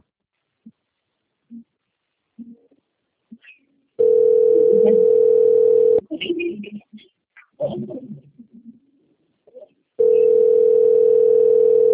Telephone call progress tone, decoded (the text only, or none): ringback tone